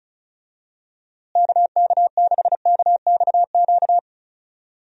Morse code, heard as a keyed tone, about 34 wpm, 700 Hz. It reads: KK6KXQ
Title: USA Call Signs - 6 Characters - Repeated 3x - 35wpm